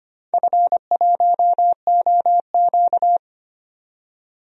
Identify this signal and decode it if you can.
Morse code, heard as a keyed tone, about 25 wpm, 700 Hz. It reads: F1OQ
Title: DXCC-100 Call Signs - 4 Characters - 25wpm